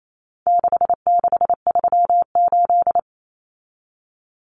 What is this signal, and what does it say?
Morse code, heard as a keyed tone, about 28 wpm, 700 Hz. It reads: 6638
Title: Numbers - 4-Digit 28wpm